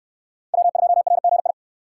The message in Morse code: W2RGI

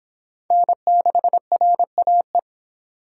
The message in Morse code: N6RAE

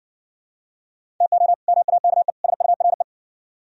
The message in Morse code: TO MAKE SURE

